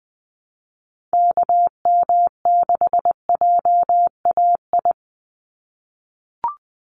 Morse code, heard as a keyed tone, about 20 wpm, 700 Hz. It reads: KM6JAI
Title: USA Call Signs 20wpm